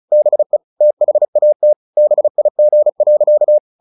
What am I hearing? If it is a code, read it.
Morse code, heard as a keyed tone, about 34 wpm, 600 Hz. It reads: BE THAT BIG.
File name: the old man and the sea 2 35wpm